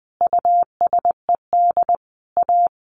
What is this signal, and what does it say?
Morse code, heard as a keyed tone, about 20 wpm, 700 Hz. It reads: USED A
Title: Sentences from Top 1100 Words 20wpm